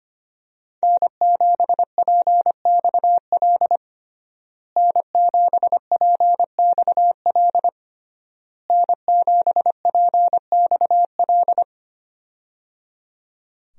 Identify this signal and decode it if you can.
Morse code, heard as a keyed tone, about 25 wpm, 700 Hz. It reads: N7PXL N7PXL N7PXL
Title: USA Call Signs - 5 Characters - Repeated 3x - 25wpm